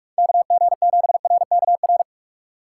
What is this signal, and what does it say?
Morse code, heard as a keyed tone, about 46 wpm, 700 Hz. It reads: KG7RKR